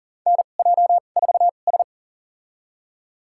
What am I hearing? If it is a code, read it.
Morse code, heard as a keyed tone, about 40 wpm, spaced as for 29 wpm, 700 Hz. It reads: NJ4S
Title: USA Call Signs - 4 Characters - 2x Character Spacing - 40wpm